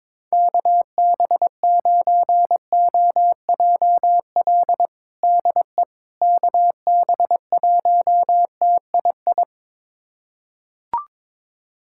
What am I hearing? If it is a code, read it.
Morse code, heard as a keyed tone, about 22 wpm, 700 Hz. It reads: KB9OJL DE KB1TII
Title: USA Callsign DE Callsign 22wpm